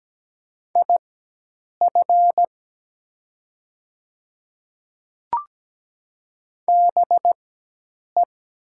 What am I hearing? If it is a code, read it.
Morse code, heard as a keyed tone, about 17 wpm, spaced as for 8 wpm, 700 Hz. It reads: IF BE
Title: 4x Character Spacing - Top 300 Words - 17wpm